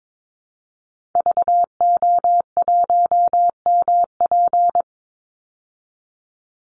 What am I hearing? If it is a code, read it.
Morse code, heard as a keyed tone, about 22 wpm, 700 Hz. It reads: VO1MP
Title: Call Signs Repeated 3x - 22wpm